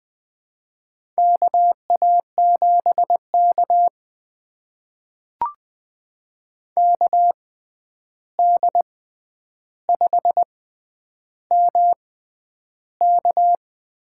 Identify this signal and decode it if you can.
Morse code, heard as a keyed tone, about 20 wpm, spaced as for 10 wpm, 700 Hz. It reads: KA7K KD5MK
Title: USA Call Signs - 6x Character Spacing - 20wpm